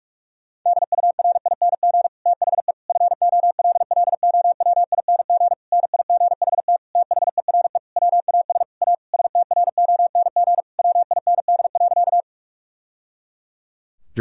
Morse code, heard as a keyed tone, about 42 wpm, 700 Hz. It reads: DURING THE FOLLOWING NIGHT THERE WAS A STRONG WIND.
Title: Sentences from Top 1000 Words - Review Part 4 - 45wpm